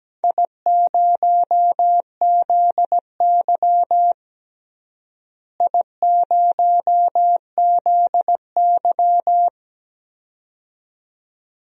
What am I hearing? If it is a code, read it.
Morse code, heard as a keyed tone, about 17 wpm, 700 Hz. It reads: I0ZY I0ZY
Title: DXCC-100 Call Signs - 4 Characters - Repeated 2x - 17wpm